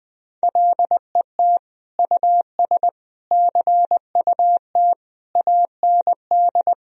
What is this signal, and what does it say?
Morse code, heard as a keyed tone, about 21 wpm, 700 Hz. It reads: LET US CUT AND
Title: Sentences from Top 1000 Words - Review Part 1 - 20wpm